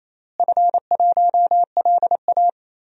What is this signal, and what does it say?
Morse code, heard as a keyed tone, about 28 wpm, 700 Hz. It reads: F1LA
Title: DXCC-100 Call Signs - 4 Characters - Repeated 3x - 28wpm